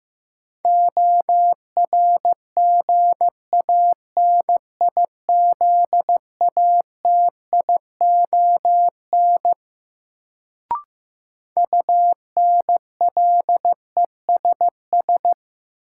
Morse code, heard as a keyed tone, about 15 wpm, 700 Hz. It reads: ORGANIZATION UNLESS
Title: Sentences from Top 1000 Words - Review Part 5 - New Word Drill - 15wpm